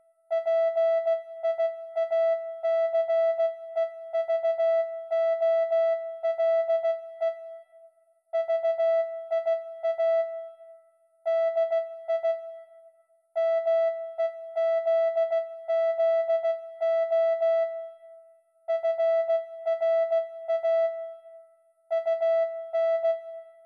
Morse code, heard as a keyed tone, about 16 wpm, 660 Hz. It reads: PIACEVOLE VIA DI MEZZO FRA UN